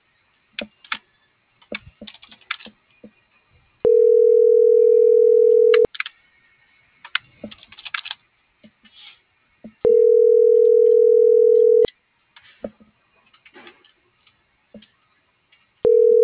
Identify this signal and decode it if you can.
Ringback tone (North America), 440 + 480 Hz.